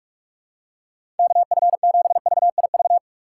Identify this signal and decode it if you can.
Morse code, heard as a keyed tone, about 46 wpm, 700 Hz. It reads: KF7VIV
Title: USA Call Signs - 6 Characters - 4x Character Spacing - 45wpm